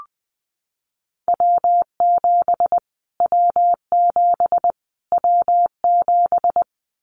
Morse code, heard as a keyed tone, about 20 wpm, 700 Hz. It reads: W7 W7 W7
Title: Callsign Prefixes for Top 25 DXCC - Repeated 3x - 20wpm